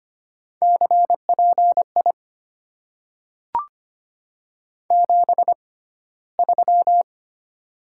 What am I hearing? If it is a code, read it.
Morse code, heard as a keyed tone, about 25 wpm, 700 Hz. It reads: C P I 7 3
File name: Top 90 QSO Elements - 6x Character Spacing - 25wpm